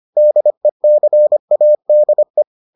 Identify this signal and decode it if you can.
Morse code, heard as a keyed tone, about 25 wpm, 600 Hz. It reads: DECADE